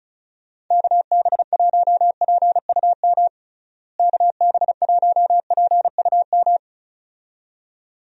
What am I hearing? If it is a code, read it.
Morse code, heard as a keyed tone, about 36 wpm, 700 Hz. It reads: KB1PUM KB1PUM
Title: USA Call Signs - 6 Characters - Repeated 2x - 35wpm